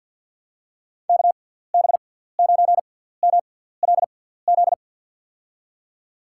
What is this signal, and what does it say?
Morse code, heard as a keyed tone, about 53 wpm, spaced as for 17 wpm, 700 Hz. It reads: KB9MLZ
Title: USA Call Signs - 6 Characters - 6x Character Spacing - 50wpm